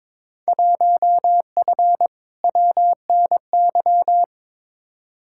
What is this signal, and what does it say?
Morse code, heard as a keyed tone, about 22 wpm, 700 Hz. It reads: 1F WNY
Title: Field Day Exchanges - 22wpm